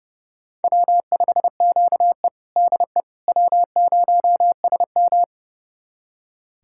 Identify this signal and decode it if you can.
Morse code, heard as a keyed tone, about 32 wpm, 700 Hz. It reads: W5QE DE W0SM
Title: USA Callsign DE Callsign - 4 Characters - 30wpm